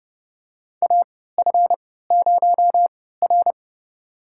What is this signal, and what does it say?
Morse code, heard as a keyed tone, about 30 wpm, spaced as for 17 wpm, 700 Hz. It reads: AF0R
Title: USA Call Signs - 3x Character Spacing - 30wpm